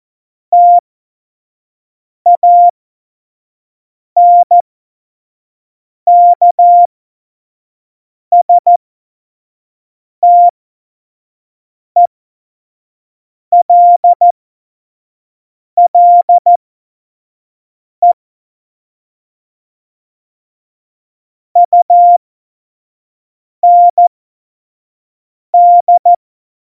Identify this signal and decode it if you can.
Morse code, heard as a keyed tone, about 14 wpm, spaced as for 5 wpm, 700 Hz. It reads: TANKSTELLE UND
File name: Morsekurs / Witze / Witz 28 Juli 2020, 25 F70